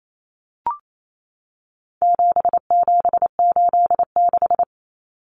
Morse code, heard as a keyed tone, about 28 wpm, 700 Hz. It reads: 7786
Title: Numbers - 4-Digit 28wpm